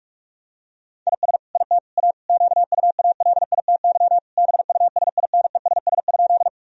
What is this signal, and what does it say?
Morse code, heard as a keyed tone, about 41 wpm, 700 Hz. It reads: IS IT A QUALITY BUSINESS?
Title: Sentences from Top 1000 Words - Review Part 4 - 45wpm